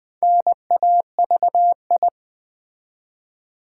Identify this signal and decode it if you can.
Morse code, heard as a keyed tone, about 20 wpm, 700 Hz. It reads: NAVI
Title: Build A Word - 4 Letters At A Time - 10-Letter Words - 20wpm